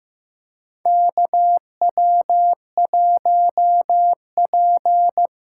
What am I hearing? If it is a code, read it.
Morse code, heard as a keyed tone, about 15 wpm, 700 Hz. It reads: KW1P
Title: USA Call Signs - 2x Character Spacing - 15wpm